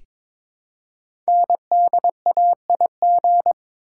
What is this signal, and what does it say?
Morse code, heard as a keyed tone, about 22 wpm, 700 Hz. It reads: NDAIG
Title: Code Groups - 5 Letters - 22wpm